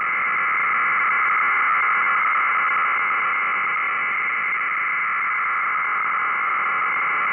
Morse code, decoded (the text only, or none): none